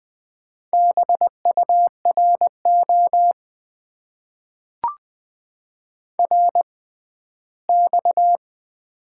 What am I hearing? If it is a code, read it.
Morse code, heard as a keyed tone, about 20 wpm, spaced as for 14 wpm, 700 Hz. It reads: BURO RX
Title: Top 90 QSO Elements - 6x Character Spacing - 20wpm